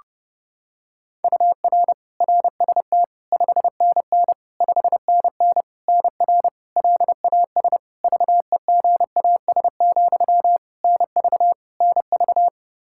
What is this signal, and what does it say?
Morse code, heard as a keyed tone, about 32 wpm, 700 Hz. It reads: UR RST 5NN 5NN NR LAS VEGAS, NV NV
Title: K1USN SST QSO Practice 30wpm